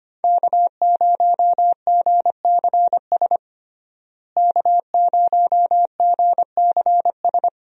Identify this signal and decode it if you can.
Morse code, heard as a keyed tone, about 25 wpm, 700 Hz. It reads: K0GCS K0GCS
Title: USA Call Signs - 5 Characters - Repeated 2x - 25wpm